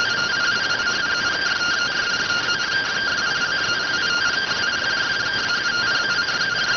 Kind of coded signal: two-tone teleprinter signal

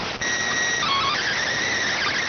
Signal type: SSTV (slow-scan television)

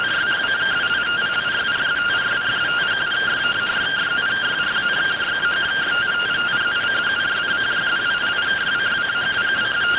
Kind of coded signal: two-tone teleprinter signal

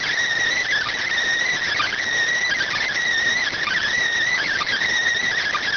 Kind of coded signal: Emergency Alert System (SAME) tones

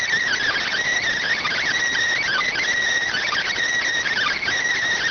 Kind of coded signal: Emergency Alert System (SAME) tones

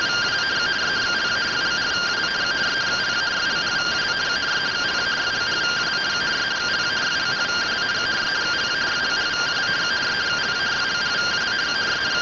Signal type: two-tone teleprinter signal

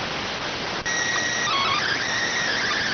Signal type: SSTV (slow-scan television)

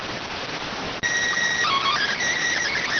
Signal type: SSTV (slow-scan television)

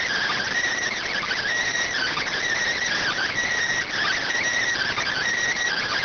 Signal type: Emergency Alert System (SAME) tones

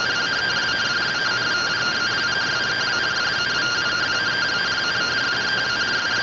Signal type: two-tone teleprinter signal

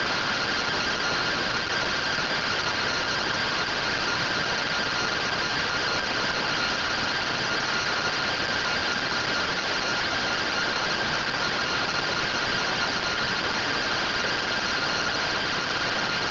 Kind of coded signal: two-tone teleprinter signal

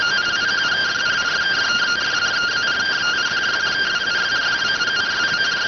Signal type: two-tone teleprinter signal

